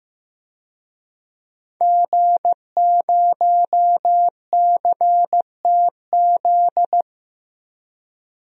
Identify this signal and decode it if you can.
Morse code, heard as a keyed tone, about 15 wpm, 700 Hz. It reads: G0CTZ
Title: DXCC-100 Call Signs - 5 Characters - 15wpm